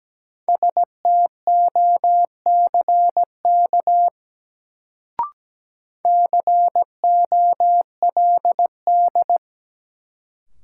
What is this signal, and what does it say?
Morse code, heard as a keyed tone, about 17 wpm, 700 Hz. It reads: STOCK COLD
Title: Sets of 5 Words - New Word Drill - 17wpm